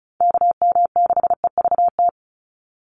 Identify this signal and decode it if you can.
Morse code, heard as a keyed tone, about 34 wpm, 700 Hz. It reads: KM6EVT
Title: USA Call Signs 35wpm